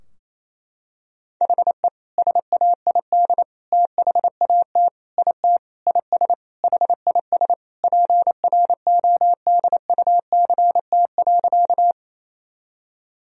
Morse code, heard as a keyed tone, about 28 wpm, 700 Hz. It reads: HE SAID THAT IT IS HIS PRODUCT.
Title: Sentences from Top 1000 Words - Review Part 2 - 28wpm